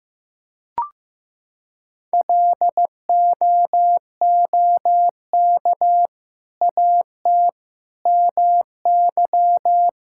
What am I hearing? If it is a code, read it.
Morse code, heard as a keyed tone, about 16 wpm, 700 Hz. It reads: LOOK AT MY
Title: Sentences from Top 1600 Words 15wpm